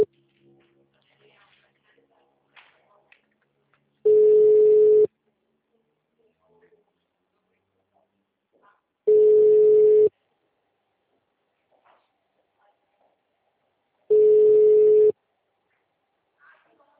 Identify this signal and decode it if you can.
Ringback tone (Europe, 425 Hz), with speech.
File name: out- -2036- - - .7844